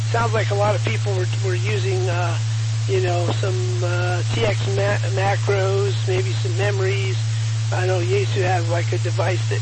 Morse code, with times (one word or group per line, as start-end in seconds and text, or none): none